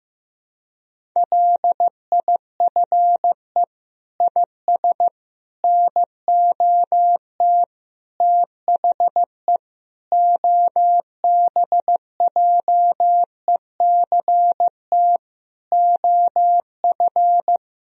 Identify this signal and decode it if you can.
Morse code, heard as a keyed tone, about 15 wpm, 700 Hz. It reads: LIFE IS NOT THE OBJECT OF
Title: Sentences from Top 1000 Words - Review Part 3 - 15wpm